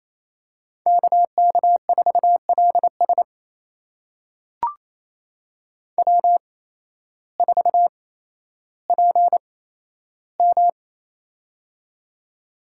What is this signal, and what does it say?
Morse code, heard as a keyed tone, about 28 wpm, spaced as for 13 wpm, 700 Hz. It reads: KK4LS W4PM
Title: USA Call Signs - 8x Character Spacing - 28wpm